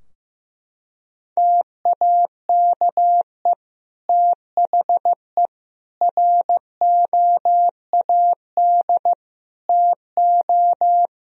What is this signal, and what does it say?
Morse code, heard as a keyed tone, about 16 wpm, 700 Hz. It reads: TAKE THE ROAD TO
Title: Sentences from Top 1000 Words - Review Part 2 - 15wpm